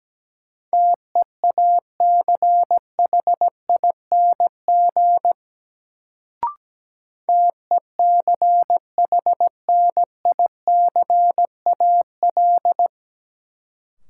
Morse code, heard as a keyed tone, about 17 wpm, 700 Hz. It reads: TEACHING TECHNICAL
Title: Sentences from Top 1100 Words - New Word Drill Alt - 17wpm